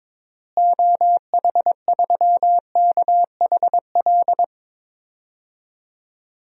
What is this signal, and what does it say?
Morse code, heard as a keyed tone, about 22 wpm, 700 Hz. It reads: OH3KHL
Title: DXCC-100 Call Signs - 6 Characters - Repeated 3x - 22wpm